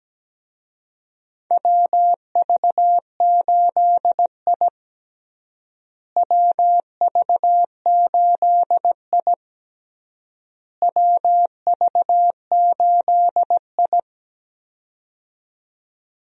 Morse code, heard as a keyed tone, about 17 wpm, 700 Hz. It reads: WV8I WV8I WV8I
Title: USA Call Signs - 4 Characters - Repeated 3x - 17wpm